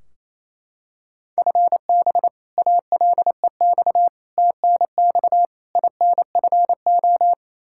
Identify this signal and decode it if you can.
Morse code, heard as a keyed tone, about 28 wpm, 700 Hz. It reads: FB ALEX TNX INFO
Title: Practice QSO - Exchange 3 - More Info - 28wpm